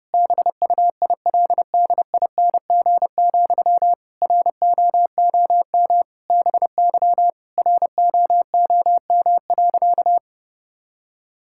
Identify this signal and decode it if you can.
Morse code, heard as a keyed tone, about 32 wpm, 700 Hz. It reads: BUILDING, ROOM BY ROOM.
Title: Sentences from Top 1000 Words - Review Part 5 - 30wpm